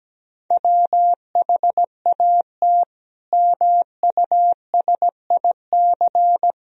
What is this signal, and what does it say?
Morse code, heard as a keyed tone, about 17 wpm, 700 Hz. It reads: WHAT MUSIC
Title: Sentences from Top 1000 Words - Review Part 1 - 17wpm